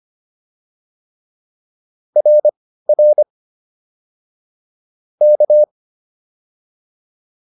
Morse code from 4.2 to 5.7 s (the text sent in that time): K